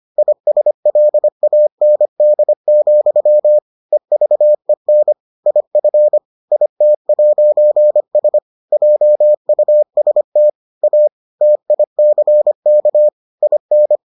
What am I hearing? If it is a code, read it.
Morse code, heard as a keyed tone, about 25 wpm, 600 Hz. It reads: ISLAND, EVEN IF IT'S JUST A TICK IN